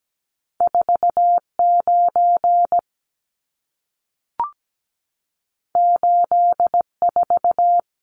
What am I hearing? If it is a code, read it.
Morse code, heard as a keyed tone, about 17 wpm, 700 Hz. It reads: 49 84
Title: Numbers - 2-Digit 17wpm